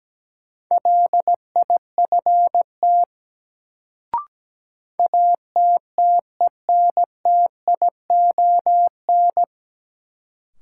Morse code, heard as a keyed tone, about 17 wpm, 700 Hz. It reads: LIFT ATTENTION
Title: Sentences from Top 1000 Words - New Word Drill - 17wpm